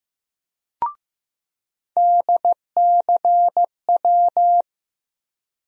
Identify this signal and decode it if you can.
Morse code, heard as a keyed tone, about 15 wpm, 700 Hz. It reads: DCW
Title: USA Call Sign Postfixes - 3 Letters 15wpm